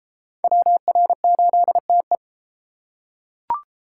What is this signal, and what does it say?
Morse code, heard as a keyed tone, about 29 wpm, 700 Hz. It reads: WR8TE